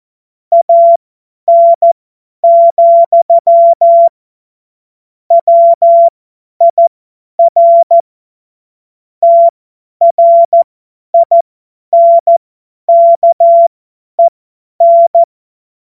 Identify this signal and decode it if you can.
Morse code, heard as a keyed tone, about 14 wpm, spaced as for 10 wpm, 680 Hz. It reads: AN, WIR TRINKEN
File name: Morsekurs / Witze / Witz 19 Dezember 2023 50 F70